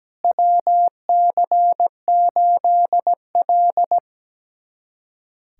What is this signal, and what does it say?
Morse code, heard as a keyed tone, about 17 wpm, 700 Hz. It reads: WC8L
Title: USA Callsign DE Callsign - 4 Characters - 17wpm